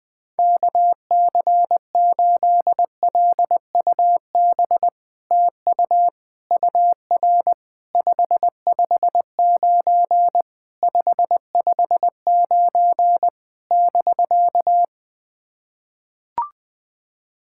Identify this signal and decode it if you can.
Morse code, heard as a keyed tone, about 20 wpm, 700 Hz. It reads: KC8LUB TU UR 559 559 <BK>
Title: POTA QSO Practice - 20wpm